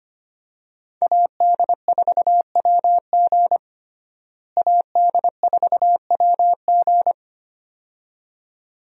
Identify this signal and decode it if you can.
Morse code, heard as a keyed tone, about 25 wpm, 700 Hz. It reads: AD4WG AD4WG
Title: USA Call Signs - 5 Characters - Repeated 2x - 25wpm